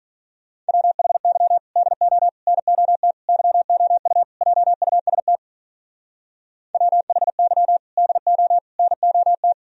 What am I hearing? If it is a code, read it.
Morse code, heard as a keyed tone, about 47 wpm, 700 Hz. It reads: WHY DO NOT YOU JUST WHY DO NOT